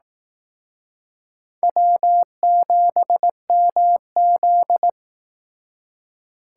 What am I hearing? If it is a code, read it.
Morse code, heard as a keyed tone, about 19 wpm, 700 Hz. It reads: W7MZ
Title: Speed Racing - USA 1x2 Call Signs 15wpm